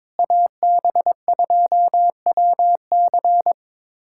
Morse code, heard as a keyed tone, about 22 wpm, 700 Hz. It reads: AB2WC
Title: USA Callsign DE Callsign - 5 Characters - 22wpm